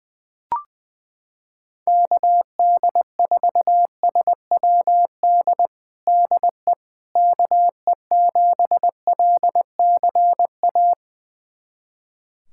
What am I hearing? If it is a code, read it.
Morse code, heard as a keyed tone, about 21 wpm, 700 Hz. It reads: KD4SWD DE KE7LCA
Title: USA Callsign DE Callsign - 6 Characters - 20wpm